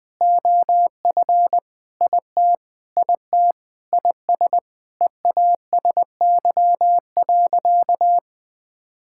Morse code, heard as a keyed tone, about 20 wpm, 700 Hz. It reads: OF IT IT IS EASY.
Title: Sentences with 7-Word Phrases - Spoken Beginning - 20wpm